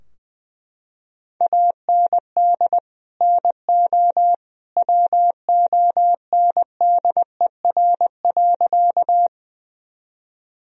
Morse code, heard as a keyed tone, about 20 wpm, 700 Hz. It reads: AND NO WONDER.
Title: Sentences from Top 1000 Words - Review Part 3 - 20wpm